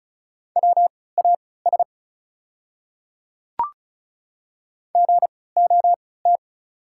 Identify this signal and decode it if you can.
Morse code, heard as a keyed tone, about 36 wpm, spaced as for 20 wpm, 690 Hz. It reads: WAS GOT
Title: 3x Character Spacing - Top 500 Words - 35wpm